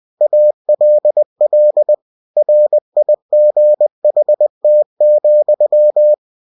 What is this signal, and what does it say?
Morse code, heard as a keyed tone, about 20 wpm, 600 Hz. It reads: ALL RIGHT,
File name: cw-books / winnie-the-pooh / winnie-20wpm-page-0097